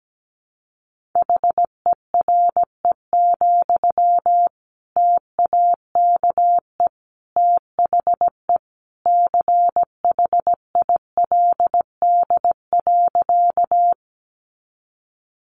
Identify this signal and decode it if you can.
Morse code, heard as a keyed tone, about 17 wpm, 700 Hz. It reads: HERE, TAKE THE CHILD.
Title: Sentences from Top 800 Words 17wpm